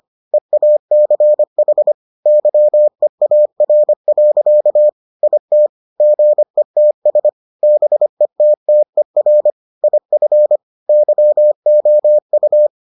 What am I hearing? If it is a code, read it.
Morse code, heard as a keyed tone, about 25 wpm, 600 Hz. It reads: EACH YEAR. IT GETS BETTER IF YOU